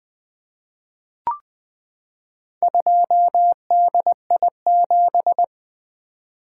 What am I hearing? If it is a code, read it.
Morse code, heard as a keyed tone, about 20 wpm, 700 Hz. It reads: 2DI7
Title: Four Letter-Number Combinations - 20wpm